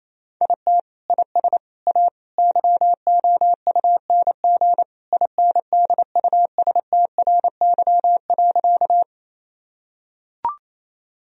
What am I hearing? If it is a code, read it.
Morse code, heard as a keyed tone, about 28 wpm, 700 Hz. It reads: IT IS A YOUNG INDUSTRY.
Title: Sentences from Top 1000 Words - Review Part 4 - 28wpm